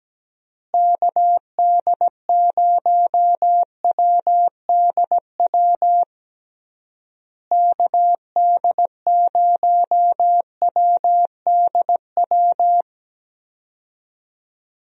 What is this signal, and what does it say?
Morse code, heard as a keyed tone, about 17 wpm, 700 Hz. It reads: KD0WDW KD0WDW
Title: USA Call Signs - 6 Characters - Repeated 3x - 17wpm